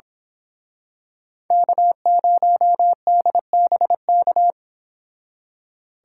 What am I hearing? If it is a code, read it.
Morse code, heard as a keyed tone, about 28 wpm, 700 Hz. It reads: K0DBK